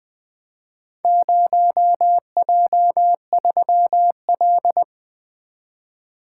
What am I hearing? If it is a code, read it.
Morse code, heard as a keyed tone, about 20 wpm, 700 Hz. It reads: 0J3L
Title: Four Letter-Number Combinations - 20wpm